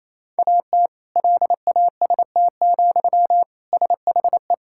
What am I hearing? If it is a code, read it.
Morse code, heard as a keyed tone, about 28 wpm, 700 Hz. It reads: AT LAST, SHE
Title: Sentences from Top 1600 Words 28wpm